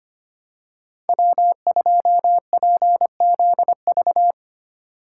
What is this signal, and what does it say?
Morse code, heard as a keyed tone, about 25 wpm, 700 Hz. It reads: W2PZV
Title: USA Call Signs - 8x Character Spacing - 25wpm